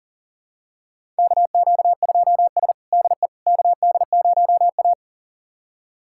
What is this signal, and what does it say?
Morse code, heard as a keyed tone, about 40 wpm, 700 Hz. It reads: KQ2S DE KD0A
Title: USA Callsign DE Callsign - 4 Characters - 40wpm